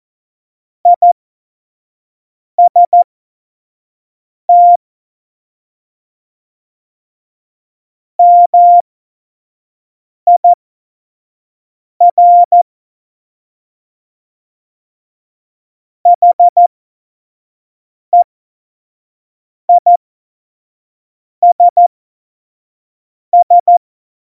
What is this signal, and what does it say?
Morse code, heard as a keyed tone, about 14 wpm, spaced as for 5 wpm, 700 Hz. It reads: IST MIR HEISS
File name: Morsekurs / Witze / Witz 17.Juli 2018, 25 F70